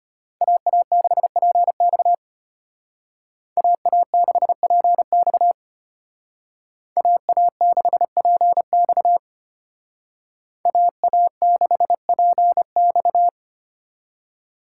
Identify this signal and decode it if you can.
Morse code, heard as a keyed tone, about 33 wpm, 700 Hz. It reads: AA6PX AA6PX AA6PX AA6PX